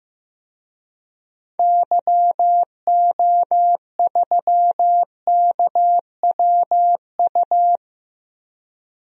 Morse code, heard as a keyed tone, about 15 wpm, 700 Hz. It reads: YO3KWU
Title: DXCC-100 Call Signs - 6 Characters - Repeated 2x - 15wpm